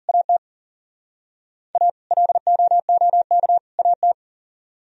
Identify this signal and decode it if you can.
Morse code, heard as a keyed tone, about 51 wpm, 700 Hz. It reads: AT A LOOK AT